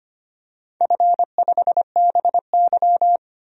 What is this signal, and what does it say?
Morse code, heard as a keyed tone, about 25 wpm, 700 Hz. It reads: F5BY
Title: DXCC-100 Call Signs - 4 Characters - 25wpm